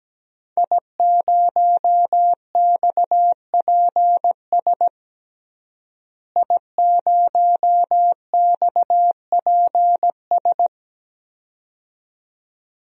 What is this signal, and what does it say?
Morse code, heard as a keyed tone, about 17 wpm, 700 Hz. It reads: I0XPS I0XPS
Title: DXCC-100 Call Signs - 5 Characters - Repeated 2x - 17wpm